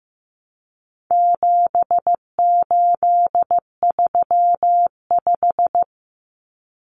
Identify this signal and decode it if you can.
Morse code, heard as a keyed tone, about 15 wpm, 700 Hz. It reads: 7835
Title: Numbers - 4-Digit 15wpm